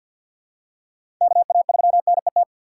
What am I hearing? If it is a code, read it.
Morse code, heard as a keyed tone, about 53 wpm, 700 Hz. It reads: KA3NET
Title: USA Call Signs - 6 Characters - 4x Character Spacing - 50wpm